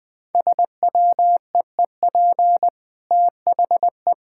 Morse code, heard as a keyed tone, about 21 wpm, 700 Hz. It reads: SWEEP THE
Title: Easy Sentences with Hints - 20wpm